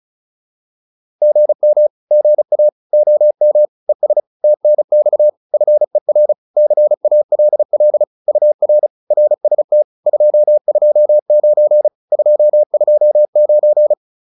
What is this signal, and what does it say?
Morse code, heard as a keyed tone, about 34 wpm, 600 Hz. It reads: GM GA OM ES TNX FER CALL UR RST 229 229
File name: QSO407 35